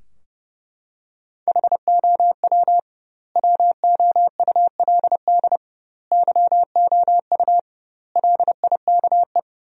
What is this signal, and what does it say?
Morse code, heard as a keyed tone, about 30 wpm, 700 Hz. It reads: HOW WOULD YOU LIKE
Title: 2x Word-Spacing - Sentences from Top 100 Words - 30wpm